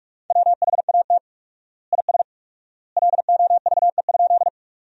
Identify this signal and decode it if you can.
Morse code, heard as a keyed tone, about 46 wpm, 700 Hz. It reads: WHAT IS LOVE?
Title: 4x Word-Spacing - Sentences from Top 400 Words - 45wpm